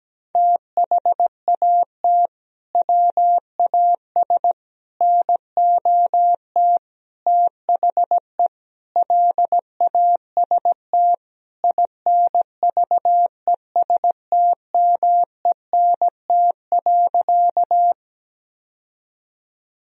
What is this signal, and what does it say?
Morse code, heard as a keyed tone, about 17 wpm, 700 Hz. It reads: THAT WAS NOT THE LAST INVESTMENT.
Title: Sentences from Top 1000 Words - Review Part 4 - 17wpm